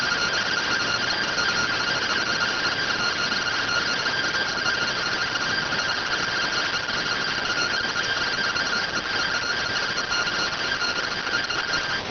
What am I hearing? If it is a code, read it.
Two-tone teleprinter signal, about 170 Hz shift.